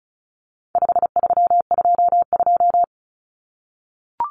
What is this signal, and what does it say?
Morse code, heard as a keyed tone, about 34 wpm, 700 Hz. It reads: 5322
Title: Common Cut Numbers - 4-Digit 35wpm